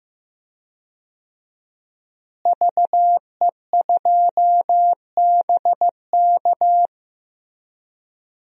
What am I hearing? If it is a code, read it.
Morse code, heard as a keyed tone, about 15 wpm, 700 Hz. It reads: VE2BK
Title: DXCC-100 Call Signs - 5 Characters - Repeated 3x - 15wpm